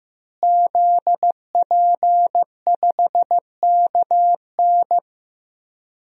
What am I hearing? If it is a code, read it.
Morse code, heard as a keyed tone, about 15 wpm, 700 Hz. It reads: ZP5KN
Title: DXCC-100 Call Signs - 5 Characters - Repeated 2x - 15wpm